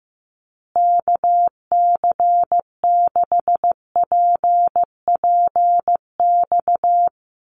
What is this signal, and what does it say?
Morse code, heard as a keyed tone, about 14 wpm, 700 Hz. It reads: KC6PPX
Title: USA Call Signs Repeated 3x 15wpm